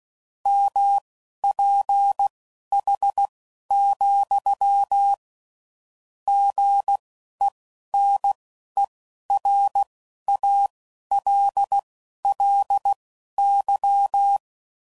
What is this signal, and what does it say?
Morse code, heard as a keyed tone, about 16 wpm, spaced as for 11 wpm, 800 Hz. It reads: MPH, GENERALLY
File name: CW Code Groups @10WPM / Weather Sets / Weather 5 12